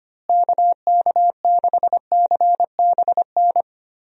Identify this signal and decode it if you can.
Morse code, heard as a keyed tone, about 25 wpm, 700 Hz. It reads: KK6CBN
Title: USA Call Signs - 6 Characters - Repeated 2x - 25wpm